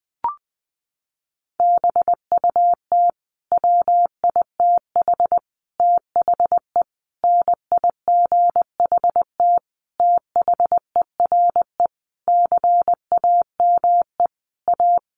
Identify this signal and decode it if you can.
Morse code, heard as a keyed tone, about 20 wpm, 700 Hz. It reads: BUT WITH THE NIGHT THERE CAME A
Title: Sentences from Top 1500 Words 20wpm